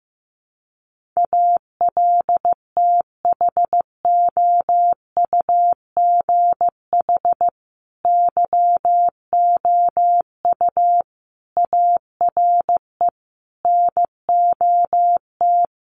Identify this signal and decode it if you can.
Morse code, heard as a keyed tone, about 15 wpm, 700 Hz. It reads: ALTHOUGH YOU ARE NOT
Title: Sentences from Top 1100 Words 15wpm